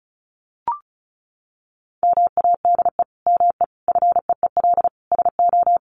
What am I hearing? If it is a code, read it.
Morse code, heard as a keyed tone, about 34 wpm, 700 Hz. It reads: MADE ME FEEL SO
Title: Sentences from Top 400 Words 35wpm